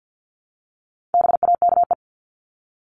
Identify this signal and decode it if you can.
Morse code, heard as a keyed tone, about 50 wpm, 700 Hz. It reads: BAKE
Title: Top 1751-2000 Words 50wpm